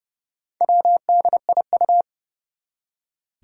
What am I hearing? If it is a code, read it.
Morse code, heard as a keyed tone, about 30 wpm, 700 Hz. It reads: WDIU
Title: Four Challenging Letter-Number Combinations 30wpm